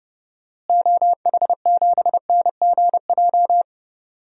Morse code, heard as a keyed tone, about 29 wpm, 700 Hz. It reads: OH7NGJ
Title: DXCC-100 Call Signs - 6 Characters - Repeated 3x - 30wpm